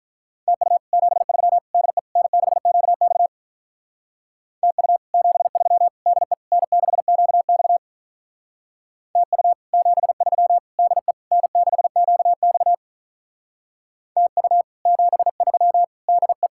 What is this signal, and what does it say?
Morse code, heard as a keyed tone, about 45 wpm, 700 Hz. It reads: TU 73 DE N6QX TU 73 DE N6QX TU 73 DE N6QX TU 73 DE